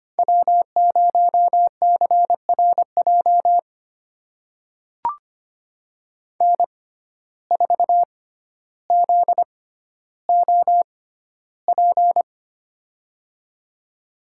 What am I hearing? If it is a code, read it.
Morse code, heard as a keyed tone, about 25 wpm, spaced as for 13 wpm, 700 Hz. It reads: W0CRJ N4ZOP
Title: USA Call Signs - 5 Characters - 6x Character Spacing - 25wpm